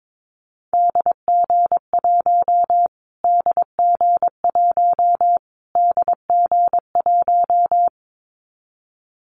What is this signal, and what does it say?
Morse code, heard as a keyed tone, about 22 wpm, 700 Hz. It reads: DG1 DG1 DG1
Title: Callsign Prefixes for Top 25 DXCC - Repeated 3x - 22wpm